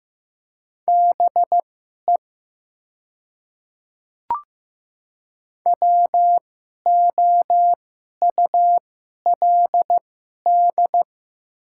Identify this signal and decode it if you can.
Morse code, heard as a keyed tone, about 15 wpm, spaced as for 11 wpm, 700 Hz. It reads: BE WOULD
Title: 2x Character Spacing - Top 100 Words - 15wpm